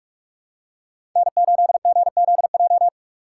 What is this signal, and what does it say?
Morse code, heard as a keyed tone, about 46 wpm, 700 Hz. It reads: N8GZJ
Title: USA Call Signs - 5 Characters - 2x Character Spacing - 45wpm